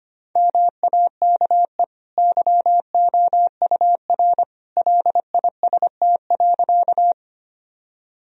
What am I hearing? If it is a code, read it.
Morse code, heard as a keyed tone, about 25 wpm, 700 Hz. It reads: MAKE YOUR LIST.
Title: Sentences from Top 1000 Words - Review Part 2 - 25wpm